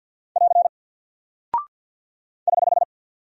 Morse code, heard as a keyed tone, about 48 wpm, 700 Hz. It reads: <AR> <HH>
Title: Single Character (Letter:Number:Punctuation:Prosign) Alt 50wpm